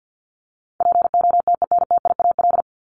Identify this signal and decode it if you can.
Morse code, heard as a keyed tone, about 54 wpm, 700 Hz. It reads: POTENTIAL
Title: Top 501-750 Words 50wpm